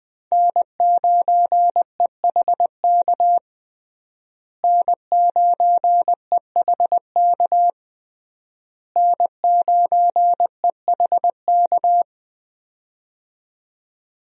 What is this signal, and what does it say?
Morse code, heard as a keyed tone, about 20 wpm, 700 Hz. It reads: N9EHK N9EHK N9EHK
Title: USA Call Signs - 5 Characters - Repeated 3x - 20wpm